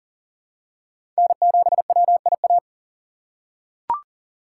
Morse code, heard as a keyed tone, about 40 wpm, 700 Hz. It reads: N7WIA
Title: USA Call Signs - 5 Characters - 3x Character Spacing - 40wpm